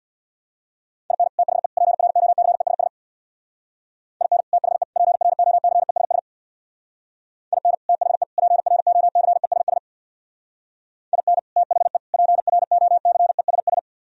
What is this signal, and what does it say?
Morse code, heard as a keyed tone, about 63 wpm, 700 Hz. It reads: IN THE PROCESS IN THE PROCESS IN THE PROCESS IN THE PROCESS